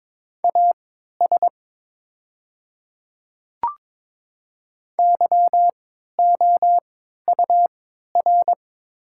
Morse code, heard as a keyed tone, about 22 wpm, spaced as for 12 wpm, 700 Hz. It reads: AS YOUR
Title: 3x Character Spacing - Top 100 Words - 22wpm